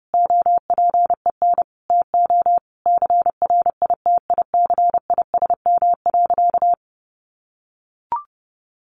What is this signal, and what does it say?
Morse code, heard as a keyed tone, about 30 wpm, 700 Hz. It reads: OPEN TO CRITICISM.
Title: Sentences from Top 1300 Words 30wpm